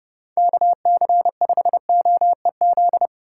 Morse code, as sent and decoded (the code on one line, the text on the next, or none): -.- -.-. ..... --- . --..
KC5OEZ